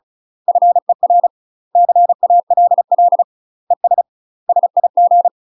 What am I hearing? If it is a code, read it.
Morse code, heard as a keyed tone, about 34 wpm, 700 Hz. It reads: FER CALL ES SIG
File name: UK MASTER QSOs TRAINING / 35WPM / UK MASTER QSOs 0039